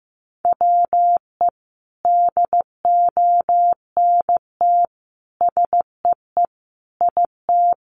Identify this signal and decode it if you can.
Morse code, heard as a keyed tone, about 15 wpm, 700 Hz. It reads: WE DONT SEE IT
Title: Sentences from Top 300 Words 15wpm